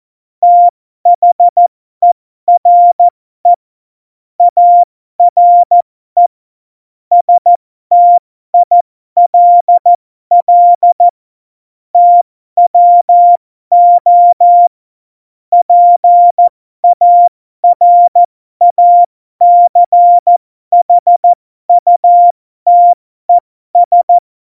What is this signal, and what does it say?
Morse code, heard as a keyed tone, about 14 wpm, spaced as for 12 wpm, 700 Hz. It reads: THERE ARE STILL TWO PARACHUTES